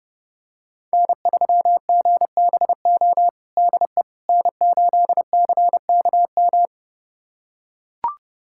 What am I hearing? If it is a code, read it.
Morse code, heard as a keyed tone, about 30 wpm, 700 Hz. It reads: N3GBO DE N8CKM
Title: USA Callsign DE Callsign - 5 Characters - 30wpm